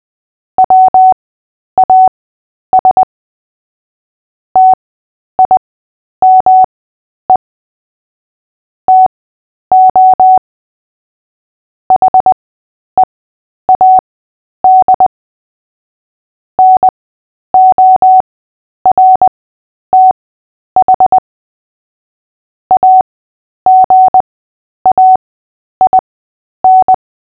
Morse code, heard as a keyed tone, about 20 wpm, spaced as for 10 wpm, 750 Hz. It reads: WAS TIME TO HEAD NORTH AGAIN
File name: Short-Story-5-10wpm